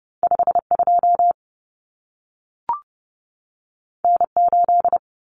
Morse code, heard as a keyed tone, about 30 wpm, 700 Hz. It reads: 52 N8
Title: Common Cut Numbers - 2-Digit 30wpm